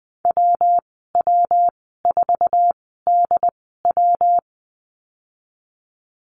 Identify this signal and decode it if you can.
Morse code, heard as a keyed tone, about 20 wpm, spaced as for 15 wpm, 700 Hz. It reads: WW4DW
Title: 122 Lesson 20g - Number 3 - Call Signs 20wpm